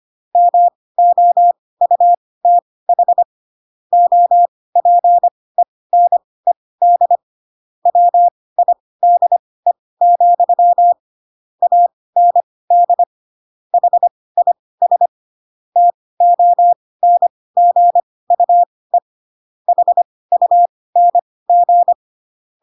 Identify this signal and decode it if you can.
Morse code, heard as a keyed tone, about 25 wpm, spaced as for 18 wpm, 700 Hz. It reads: MOUTH OPENED WIDE, AND HIS TONGUE HUNG
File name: Pinocchio / Pinocchio ch1 12 25 ik1hge